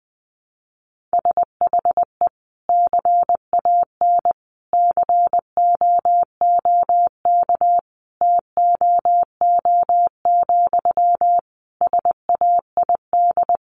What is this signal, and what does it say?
Morse code, heard as a keyed tone, about 20 wpm, 700 Hz. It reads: SHE CAN COOK TOO, SAID
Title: Sentences from Top 700 Words 20wpm